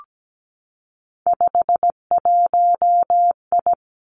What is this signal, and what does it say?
Morse code, heard as a keyed tone, about 17 wpm, 700 Hz. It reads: 51I
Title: Three Challenging Letter-Number Combinations 17wpm